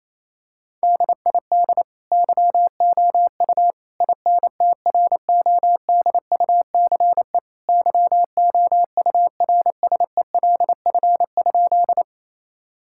Morse code, heard as a keyed tone, about 28 wpm, 700 Hz. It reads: DID YOU INTRODUCE YOURSELF?
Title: Sentences from Top 1600 Words 28wpm